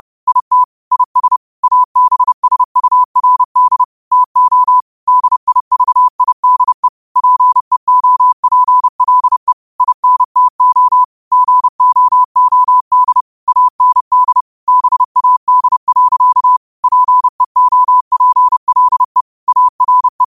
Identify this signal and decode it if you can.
Morse code, heard as a keyed tone, about 30 wpm, 1000 Hz. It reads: IT IS ABSURD TO DIVIDE PEOPLE INTO GOOD AND BAD. PEOPLE ARE
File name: qotdfiles / 30wpm